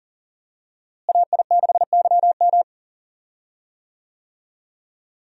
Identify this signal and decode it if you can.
Morse code, heard as a keyed tone, about 37 wpm, 700 Hz. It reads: AI6YM
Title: USA Call Signs - 5 Characters - Repeated 3x - 40wpm